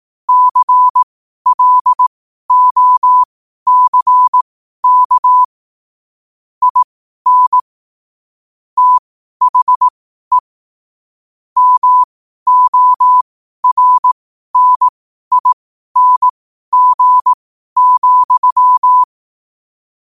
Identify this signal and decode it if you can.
Morse code, heard as a keyed tone, about 18 wpm, spaced as for 12 wpm, 1000 Hz. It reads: CLOCK IN THE MORNING,
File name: qotdfiles / 12wpm